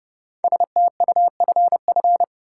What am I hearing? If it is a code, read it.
Morse code, heard as a keyed tone, about 30 wpm, 700 Hz. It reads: STUFF
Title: Sentences from Top 1400 Words - New Word Drill - 30wpm